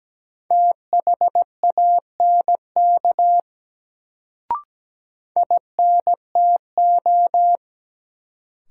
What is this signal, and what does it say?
Morse code, heard as a keyed tone, about 17 wpm, 700 Hz. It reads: THANK INTO
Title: Sentences from Top 600 Words - New Word Drill - 17wpm